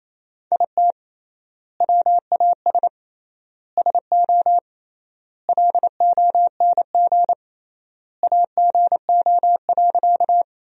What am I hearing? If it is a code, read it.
Morse code, heard as a keyed tone, about 28 wpm, 700 Hz. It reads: IT WAS SO LONG AGO.
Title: 3x Word-Spacing - Sentences from Top 500 Words - 28wpm